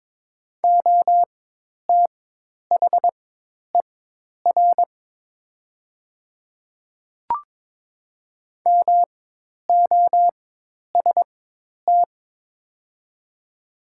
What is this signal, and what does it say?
Morse code, heard as a keyed tone, about 22 wpm, spaced as for 10 wpm, 690 Hz. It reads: OTHER MOST
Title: 4x Character Spacing - Top 100 Words - 22wpm